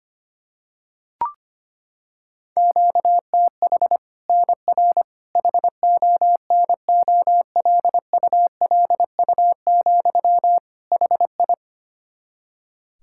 Morse code, heard as a keyed tone, about 25 wpm, 700 Hz. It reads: QTH NR HONOLULU, HI
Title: Practice QSO - Exchange 1 - Beginning - 25wpm